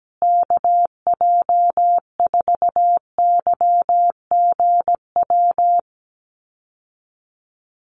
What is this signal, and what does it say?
Morse code, heard as a keyed tone, about 17 wpm, 700 Hz. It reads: KJ4YGW
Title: USA Call Signs Repeated 3x 17wpm